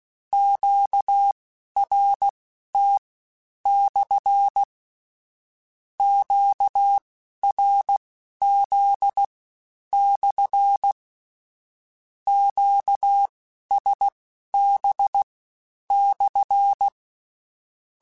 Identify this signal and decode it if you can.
Morse code, heard as a keyed tone, about 16 wpm, spaced as for 10 wpm, 800 Hz. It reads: QRT/ QRZ/ QSB/
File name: Q Calls / Q Calls-mp3s / Q Calls SPSL1 @15